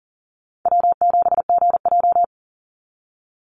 Morse code, heard as a keyed tone, about 40 wpm, 700 Hz. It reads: W7GJ
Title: NA SOTA Chaser Callsigns 40wpm